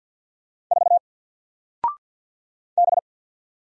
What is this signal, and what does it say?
Morse code, heard as a keyed tone, about 50 wpm, 700 Hz. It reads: 4 B
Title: Single Letter-Number 50wpm